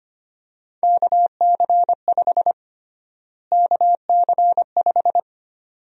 Morse code, heard as a keyed tone, about 25 wpm, 700 Hz. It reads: KC5 KC5
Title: USA Call Sign Prefixes - Repeated 2x - 25wpm